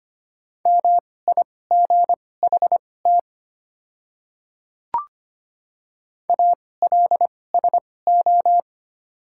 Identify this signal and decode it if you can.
Morse code, heard as a keyed tone, about 25 wpm, spaced as for 18 wpm, 700 Hz. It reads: MIGHT ALSO
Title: 2x Character Spacing - Top 300 Words - 25wpm